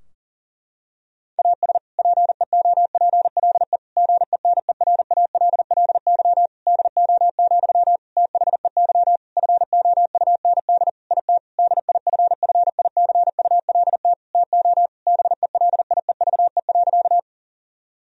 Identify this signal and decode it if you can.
Morse code, heard as a keyed tone, about 40 wpm, 700 Hz. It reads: AS PEOPLE GENERALLY DO, THEY FOUND IT DIFFICULT TO BELIEVE.
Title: Sentences from Top 1000 Words - Review Part 4 - 40wpm